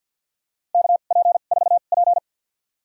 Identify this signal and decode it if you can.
Morse code, heard as a keyed tone, about 52 wpm, spaced as for 36 wpm, 700 Hz. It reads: KP4P
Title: USA Call Signs - 4 Characters - 2x Character Spacing - 50wpm